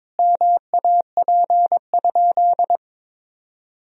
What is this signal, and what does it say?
Morse code, heard as a keyed tone, about 22 wpm, 700 Hz. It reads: MAP?
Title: Sentences from Top 1000 Words - Review Part 2 - 22wpm